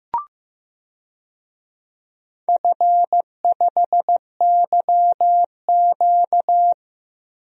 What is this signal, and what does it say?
Morse code, heard as a keyed tone, about 15 wpm, 700 Hz. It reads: F5YQ
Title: DXCC-100 Call Signs - 4 Characters - Repeated 3x - 15wpm